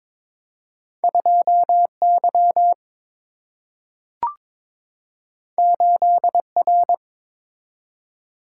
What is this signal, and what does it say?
Morse code, heard as a keyed tone, about 22 wpm, 700 Hz. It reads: 2 Y 8 R
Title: Single-Letter Single-Number 22wpm